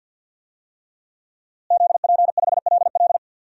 Morse code, heard as a keyed tone, about 46 wpm, 700 Hz. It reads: ZP5LL
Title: DXCC-100 Call Signs - 5 Characters - Repeated 3x - 50wpm